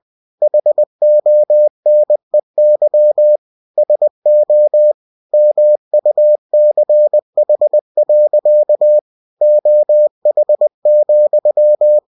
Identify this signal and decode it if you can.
Morse code, heard as a keyed tone, about 20 wpm, 600 Hz. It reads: HONEY SO MUCH. OH,
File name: cw-books / winnie-the-pooh / winnie-20wpm-page-0004